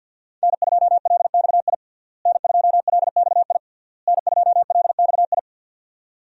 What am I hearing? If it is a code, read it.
Morse code, heard as a keyed tone, about 51 wpm, 700 Hz. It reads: N2LXI N2LXI N2LXI
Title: USA Call Signs - 5 Characters - Repeated 3x - 50wpm